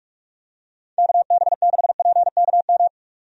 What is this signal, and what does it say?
Morse code, heard as a keyed tone, about 46 wpm, 700 Hz. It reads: KB6PKM
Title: USA Call Signs - 6 Characters - 45wpm